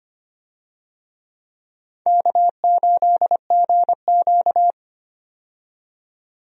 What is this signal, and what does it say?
Morse code, heard as a keyed tone, about 25 wpm, 700 Hz. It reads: K8GQ
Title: USA Call Signs - 4 Characters - 25wpm